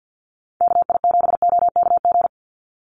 Morse code, heard as a keyed tone, about 48 wpm, 700 Hz. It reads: KI7OKG
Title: USA Call Signs 50wpm